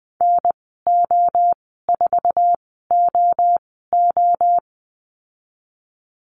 Morse code, heard as a keyed tone, about 20 wpm, spaced as for 15 wpm, 700 Hz. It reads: NO4OO
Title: 048 Lesson 9g - Number 4 - Call Signs 20wpm